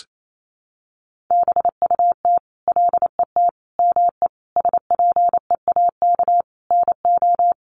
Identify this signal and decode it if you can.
Morse code, heard as a keyed tone, about 28 wpm, 700 Hz. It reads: BUT LET ME SPEAK NO
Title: Sentences from Top 1100 Words 28wpm